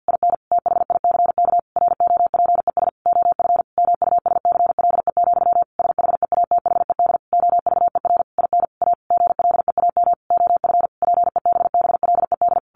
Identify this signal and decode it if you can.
Morse code, heard as a keyed tone, about 50 wpm, 700 Hz. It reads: IN THICK ROPES OF MUSCLE, SHEATHED OVER IN A GLEAM OF PEBBLED
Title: A Sound Of Thunder 50wpm